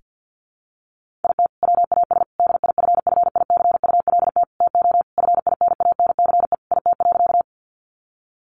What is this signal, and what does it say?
Morse code, heard as a keyed tone, about 50 wpm, 700 Hz. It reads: IT WAS DIFFICULT TO FINANCE IT.
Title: Sentences from Top 1500 Words 50wpm